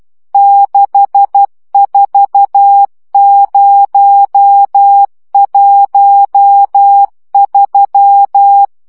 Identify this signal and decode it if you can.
Morse code, heard as a keyed tone, about 12 wpm, 800 Hz. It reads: 64013
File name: morze vizsga 2011-2 10